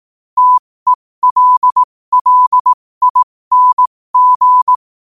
Morse code, heard as a keyed tone, about 18 wpm, spaced as for 15 wpm, 1000 Hz. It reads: TELLING
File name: qotdfiles / 15wpm